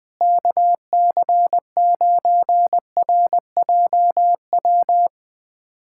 Morse code, heard as a keyed tone, about 21 wpm, 700 Hz. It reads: KC9RJW
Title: USA Callsign DE Callsign - 6 Characters - 20wpm